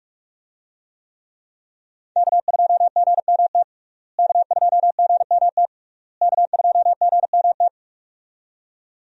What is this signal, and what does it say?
Morse code, heard as a keyed tone, about 42 wpm, 700 Hz. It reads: K2GMT K2GMT K2GMT
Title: USA Call Signs - 5 Characters - Repeated 3x - 45wpm